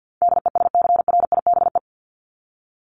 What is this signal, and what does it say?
Morse code, heard as a keyed tone, about 50 wpm, 700 Hz. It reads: DESCRIBE
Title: Top 751-1000 Words 50wpm